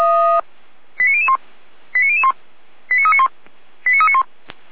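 Five-tone selective call (ZVEI), heard as quick runs of five tones, tones of about 80 ms. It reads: 89001 89001 89391 89391
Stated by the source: ZVEI five-tone selective call received on SDR.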